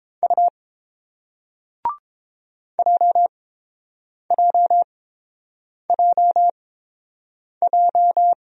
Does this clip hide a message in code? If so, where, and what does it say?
Yes: Morse code from 0.2 to 8.3 s, heard as a keyed tone, about 30 wpm, spaced as for 7 wpm, 700 Hz. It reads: U JJJJ